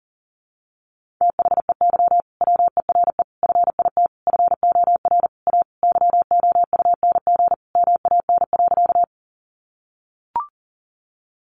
Morse code, heard as a keyed tone, about 40 wpm, 700 Hz. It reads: THEY WERE FIT FOR A YOUNG MAN.
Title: Sentences from Top 800 Words 40wpm